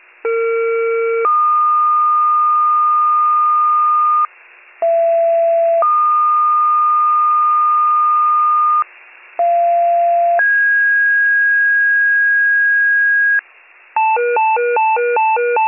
Two-tone page, heard as a one-second tone, then a three-second tone, about 460 then 1220 Hz, about 670 then 1220 Hz, about 690 then 1650 Hz, with noise. It is heard from 0.2 to 13.4 s.